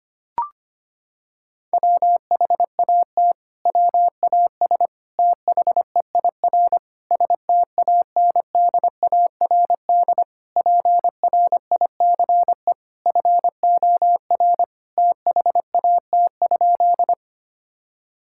Morse code, heard as a keyed tone, about 25 wpm, 700 Hz. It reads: WHAT WAS THEIR STANDARD PRICE FOR THAT?
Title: Sentences from Top 1000 Words - Review Part 6 - 25wpm